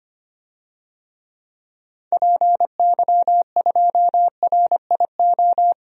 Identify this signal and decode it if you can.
Morse code, heard as a keyed tone, about 25 wpm, 700 Hz. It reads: PY2RIO
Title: DXCC-100 Call Signs - 6 Characters - Repeated 3x - 25wpm